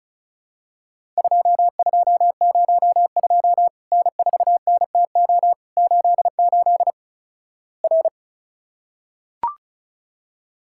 Morse code, heard as a keyed tone, about 36 wpm, 700 Hz. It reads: 2202 N4NTO 88 R
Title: Worked All Europe DX Contest - QTC Exchange - 35wpm